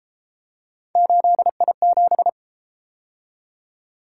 Morse code, heard as a keyed tone, about 32 wpm, 700 Hz. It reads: 8I7